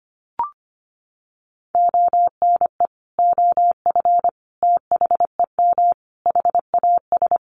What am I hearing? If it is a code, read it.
Morse code, heard as a keyed tone, about 25 wpm, 700 Hz. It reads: ONE OF THEM HAS
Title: Additional Easy Phrases 25wpm